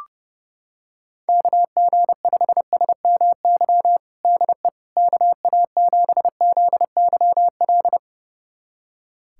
Morse code, heard as a keyed tone, about 30 wpm, 700 Hz. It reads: KG5SMY DE KA7ZYL
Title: USA Callsign DE Callsign - 6 Characters - 30wpm